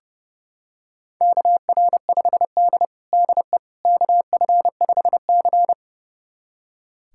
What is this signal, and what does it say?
Morse code, heard as a keyed tone, about 30 wpm, 700 Hz. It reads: KR5D DE KF5C
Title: USA Callsign DE Callsign - 4 Characters - 30wpm